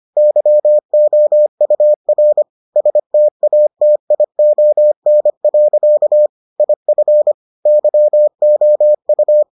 Morse code, heard as a keyed tone, about 25 wpm, 600 Hz. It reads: YOUR STATION. IF YOU